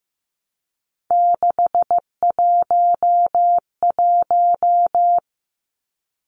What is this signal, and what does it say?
Morse code, heard as a keyed tone, about 15 wpm, 700 Hz. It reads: 611
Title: Numbers - 3-Digit 15wpm